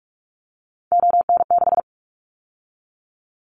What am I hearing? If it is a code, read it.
Morse code, heard as a keyed tone, about 46 wpm, 700 Hz. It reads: ON6
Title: Callsign Prefixes for Top 50 DXCC - 45wpm